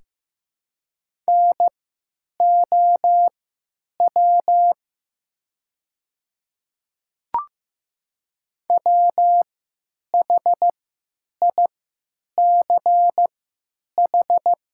Morse code, heard as a keyed tone, about 15 wpm, spaced as for 9 wpm, 700 Hz. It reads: NOW WHICH
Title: 3x Character Spacing - Top 100 Words - 15wpm